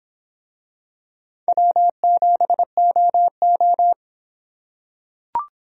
Morse code, heard as a keyed tone, about 26 wpm, 700 Hz. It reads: W7OO